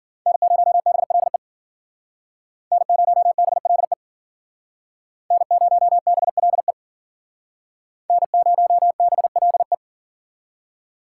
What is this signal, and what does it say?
Morse code, heard as a keyed tone, about 52 wpm, 700 Hz. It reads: N0BLE N0BLE N0BLE N0BLE